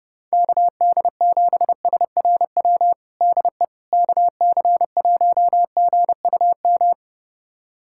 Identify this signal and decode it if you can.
Morse code, heard as a keyed tone, about 30 wpm, 700 Hz. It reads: KD7SRW DE KC1GUM
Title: USA Callsign DE Callsign - 6 Characters - 30wpm